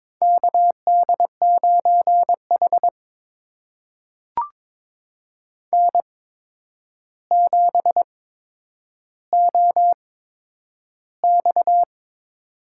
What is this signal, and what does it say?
Morse code, heard as a keyed tone, about 22 wpm, spaced as for 9 wpm, 700 Hz. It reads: KD9H N7OX
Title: USA Call Signs - 4 Characters - 8x Character Spacing - 22wpm